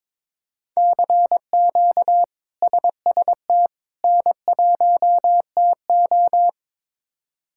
Morse code, heard as a keyed tone, about 22 wpm, 700 Hz. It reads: CQ SST N1TO
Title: K1USN SST QSO Practice - 22wpm